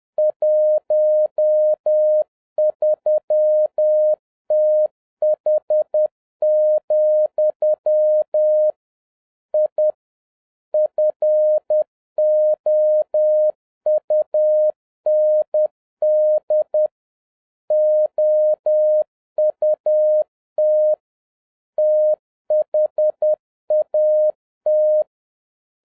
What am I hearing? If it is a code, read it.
Morse code, heard as a keyed tone, about 10 wpm, 610 Hz. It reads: 13TH, I FOUND OUT THAT
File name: 2011-07-mp3 / n36 THE 3.11 2 10wpm